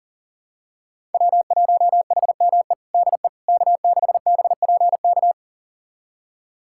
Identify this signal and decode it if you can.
Morse code, heard as a keyed tone, about 37 wpm, 700 Hz. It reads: W1HME DE K6BPK
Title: USA Callsign DE Callsign - 5 Characters - 40wpm